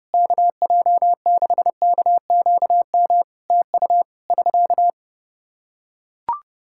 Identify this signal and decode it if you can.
Morse code, heard as a keyed tone, about 30 wpm, 700 Hz. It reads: KJ6KQM TU <SK>
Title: Practice QSO - Exchange 4 - Ending - 30wpm